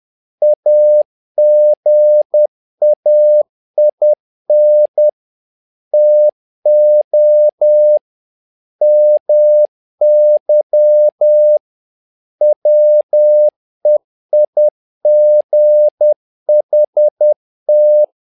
Morse code, heard as a keyed tone, about 10 wpm, 600 Hz. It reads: AGAIN TO MY WEIGHT
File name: learning center / tower trouble / tower trouble 10wpm